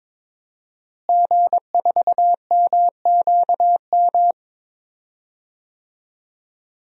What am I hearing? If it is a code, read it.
Morse code, heard as a keyed tone, about 23 wpm, 700 Hz. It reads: G4MQM
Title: DXCC-100 Call Signs - 5 Characters - Repeated 3x - 22wpm